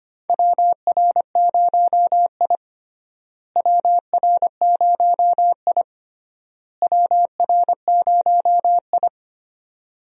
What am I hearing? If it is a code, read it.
Morse code, heard as a keyed tone, about 25 wpm, 700 Hz. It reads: WR0I WR0I WR0I
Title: USA Call Signs - 4 Characters - Repeated 3x - 25wpm